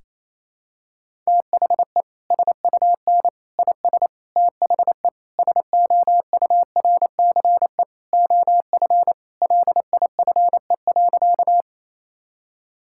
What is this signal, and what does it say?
Morse code, heard as a keyed tone, about 28 wpm, 700 Hz. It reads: THE SUN IS THE SOURCE OF LIFE.
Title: Sentences from Top 1000 Words - Review Part 5 - 28wpm